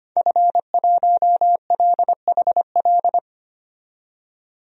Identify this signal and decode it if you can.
Morse code, heard as a keyed tone, about 25 wpm, 700 Hz. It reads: F1LHL
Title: DXCC-100 Call Signs - 5 Characters - 25wpm